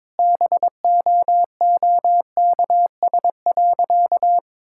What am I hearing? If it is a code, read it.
Morse code, heard as a keyed tone, about 22 wpm, 700 Hz. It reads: BOOKS.
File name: Difficult Sentences with Hints - 22wpm